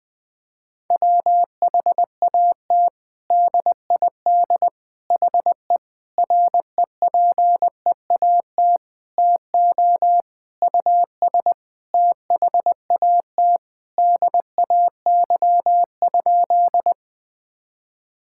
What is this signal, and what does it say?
Morse code, heard as a keyed tone, about 21 wpm, 700 Hz. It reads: WHAT DID HE REPEAT TO US THAT DAY?
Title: Sentences from Top 1600 Words 20wpm